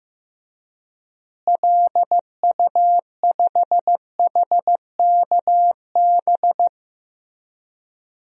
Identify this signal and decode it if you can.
Morse code, heard as a keyed tone, about 15 wpm, 700 Hz. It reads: LU5HKB
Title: DXCC-100 Call Signs - 6 Characters - Repeated 3x - 15wpm